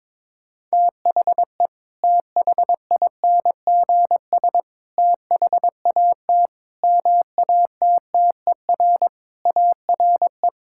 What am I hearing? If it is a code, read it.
Morse code, heard as a keyed tone, about 22 wpm, 700 Hz. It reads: THE THINGS THAT MATTER ARE
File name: practice / 22wpm / Sentences with 3-Word Phrases - 22wpm